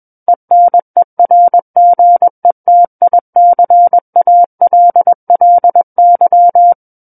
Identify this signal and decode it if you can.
Morse code, heard as a keyed tone, about 21 wpm, 700 Hz. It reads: ENERGETICALLY
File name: Book One / 12-Chapter 12 - 21 WPM